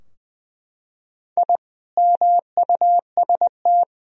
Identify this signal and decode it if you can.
Morse code, heard as a keyed tone, about 21 wpm, 700 Hz. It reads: I MUST
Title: Sentences from Top 1000 Words - Review Part 3 - 20wpm